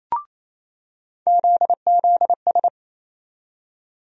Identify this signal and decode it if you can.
Morse code, heard as a keyed tone, about 28 wpm, 700 Hz. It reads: ZZS
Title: USA Call Sign Postfixes - 3 Letters 28wpm